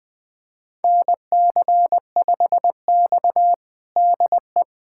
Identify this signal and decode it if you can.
Morse code, heard as a keyed tone, about 20 wpm, 700 Hz. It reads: NC5X DE
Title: USA Callsign DE Callsign - 4 Characters - 20wpm